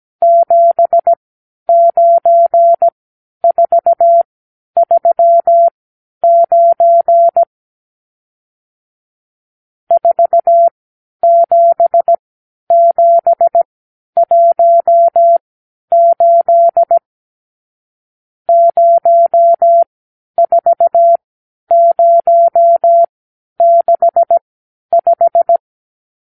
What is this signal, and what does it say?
Morse code, heard as a keyed tone, about 17 wpm, spaced as for 10 wpm, 680 Hz. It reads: 79439 47718 04065
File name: arrl / Review (1-0)